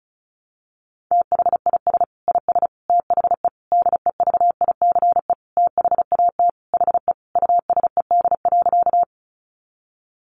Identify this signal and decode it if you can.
Morse code, heard as a keyed tone, about 34 wpm, 700 Hz. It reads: THIS IS THE DEVICE THAT HE USED.
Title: Sentences from Top 1500 Words 35wpm